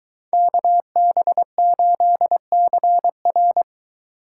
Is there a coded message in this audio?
Yes: Morse code from 0.3 to 3.6 s, heard as a keyed tone, about 23 wpm, 700 Hz. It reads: KB8CR